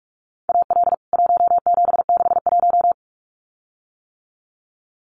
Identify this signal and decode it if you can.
Morse code, heard as a keyed tone, about 44 wpm, 700 Hz. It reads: AL 1761
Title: CWT Exchange 45wpm